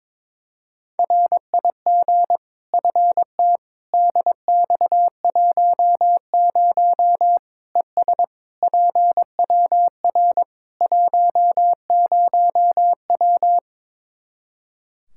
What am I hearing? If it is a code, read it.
Morse code, heard as a keyed tone, about 21 wpm, 700 Hz. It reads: RIG FT DX10 ES PWR 10W
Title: Practice QSO - Exchange 2 - Info - 22wpm